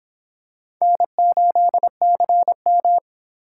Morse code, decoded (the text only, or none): N8CM